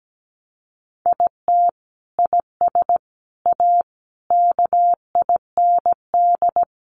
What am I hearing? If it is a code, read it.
Morse code, heard as a keyed tone, about 17 wpm, 700 Hz. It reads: IT IS A KIND
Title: Sentences from Top 1200 Words 17wpm